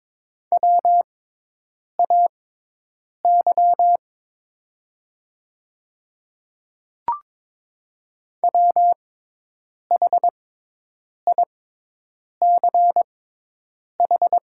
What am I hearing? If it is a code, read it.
Morse code, heard as a keyed tone, about 22 wpm, spaced as for 8 wpm, 700 Hz. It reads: WAY WHICH
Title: 6x Character Spacing - Top 100 Words - 22wpm